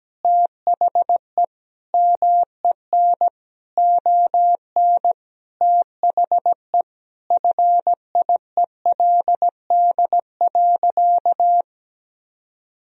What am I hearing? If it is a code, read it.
Morse code, heard as a keyed tone, about 17 wpm, 700 Hz. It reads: THE MEN ON THE FIELD.
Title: Sentences from Top 1000 Words - Review Part 2 - 17wpm